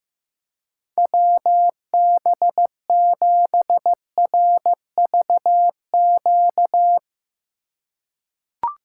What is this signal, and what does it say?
Morse code, heard as a keyed tone, about 16 wpm, 700 Hz. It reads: WB7RVQ
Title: USA Call Signs - 6 Characters - 2x Character Spacing - 15wpm